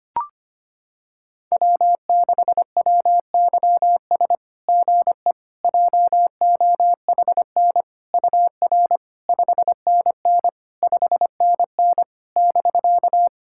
Morse code, heard as a keyed tone, about 25 wpm, 690 Hz. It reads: W6WYS GE JOHN UR 5NN 5NN <BK>
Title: SOTA QSO Practice 25wpm